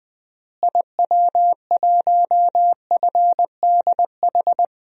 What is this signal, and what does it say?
Morse code, heard as a keyed tone, about 20 wpm, 700 Hz. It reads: IW1FDH
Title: DXCC-100 Call Signs - 6 Characters - Repeated 3x - 20wpm